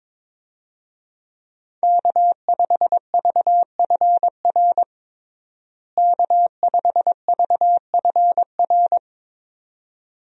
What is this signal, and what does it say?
Morse code, heard as a keyed tone, about 22 wpm, 700 Hz. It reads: K5VFR K5VFR
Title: USA Call Signs - 5 Characters - Repeated 2x - 22wpm